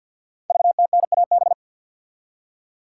Morse code, heard as a keyed tone, about 48 wpm, 700 Hz. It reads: VTNAB
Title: Code Groups - 5 Letters - 50wpm